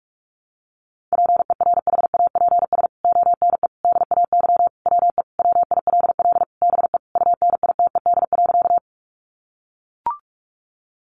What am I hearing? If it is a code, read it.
Morse code, heard as a keyed tone, about 44 wpm, 700 Hz. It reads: PERHAPS ONE DAY WE WILL BE UNITED.
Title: Sentences from Top 600 Words 45wpm